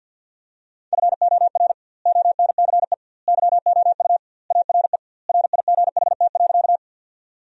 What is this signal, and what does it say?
Morse code, heard as a keyed tone, about 53 wpm, 700 Hz. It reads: FOR ONCE YOU ARE RIGHT.
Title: 2x Word-Spacing - Sentences from Top 300 Words - 50wpm